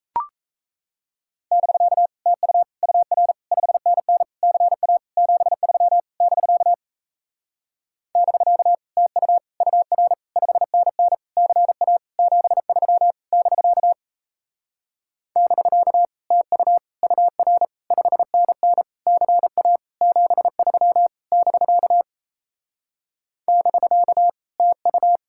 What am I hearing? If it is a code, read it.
Morse code, heard as a keyed tone, about 38 wpm, 700 Hz. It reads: <BK> TU UR 5NN CA 73 <BK> <BK> TU UR 5NN CA 73 <BK> <BK> TU UR 5NN CA 73 <BK> <BK> TU